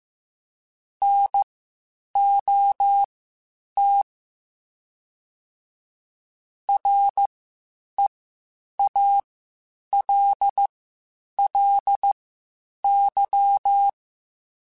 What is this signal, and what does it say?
Morse code, heard as a keyed tone, about 15 wpm, spaced as for 8 wpm, 800 Hz. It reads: NOT REALLY
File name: (spaced) / QSOs Spaced-mp3s / WD0BGP 1 @13 sp